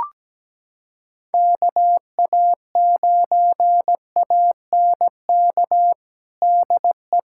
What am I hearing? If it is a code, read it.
Morse code, heard as a keyed tone, about 17 wpm, 700 Hz. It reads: KA9ANK DE
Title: USA Callsign DE Callsign - 6 Characters - 17wpm